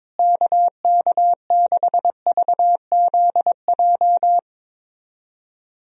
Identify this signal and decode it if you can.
Morse code, heard as a keyed tone, about 21 wpm, 700 Hz. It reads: KK6VZJ
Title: USA Call Signs - 6 Characters - Repeated 2x - 22wpm